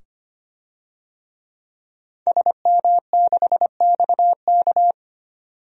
Morse code, heard as a keyed tone, about 25 wpm, 700 Hz. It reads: SM6XK
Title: DXCC-100 Call Signs - 5 Characters - 25wpm